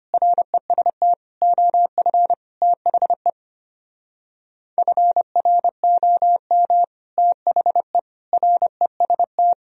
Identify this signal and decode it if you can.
Morse code, heard as a keyed tone, about 26 wpm, 700 Hz. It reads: REST OF THE FROM THE REST
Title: Speed-Racing - Sets of 5 Words 25wpm